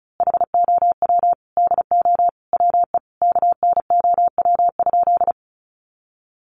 Morse code, heard as a keyed tone, about 34 wpm, 700 Hz. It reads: HOW DO WE KNOW?
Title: A Sound Of Thunder 35wpm